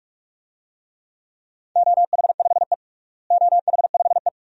Morse code, heard as a keyed tone, about 42 wpm, 700 Hz. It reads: OH5E OH5E
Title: DXCC-100 Call Signs - 4 Characters - Repeated 2x - 45wpm